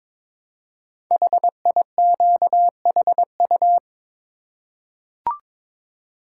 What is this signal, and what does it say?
Morse code, heard as a keyed tone, about 22 wpm, 700 Hz. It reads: HIQHU
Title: Code Groups - 5 Letters - 22wpm